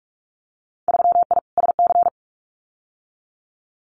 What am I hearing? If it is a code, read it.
Morse code, heard as a keyed tone, about 44 wpm, 700 Hz. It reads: 3I SC
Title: Winter Field Day Exchanges 45wpm